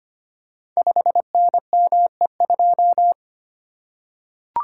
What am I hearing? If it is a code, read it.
Morse code, heard as a keyed tone, about 25 wpm, 700 Hz. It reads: 5NME2
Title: Code Groups - 5 Letter:Numbers - 25wpm